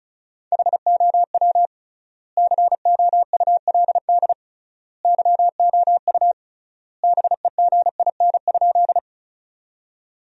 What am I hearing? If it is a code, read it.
Morse code, heard as a keyed tone, about 36 wpm, 700 Hz. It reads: HOW COULD YOU BEGIN?
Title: 3x Word-Spacing - Sentences from Top 300 Words - 35wpm